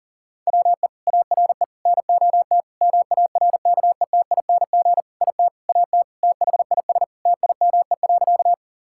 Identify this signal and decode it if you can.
Morse code, heard as a keyed tone, about 40 wpm, 700 Hz. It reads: WE ARE NOT MARKETING IT AT THIS TIME.
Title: Sentences from Top 1000 Words - Review Part 5 - 40wpm